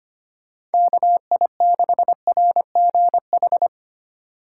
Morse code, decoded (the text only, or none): KI6RGH